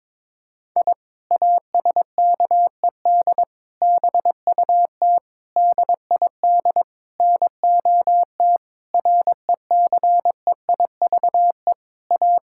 Morse code, heard as a keyed tone, about 21 wpm, 700 Hz. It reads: I ASKED BUT DID NOT RECEIVE A
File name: practice / 22wpm / Sentences with 8-Word Phrases - 22wpm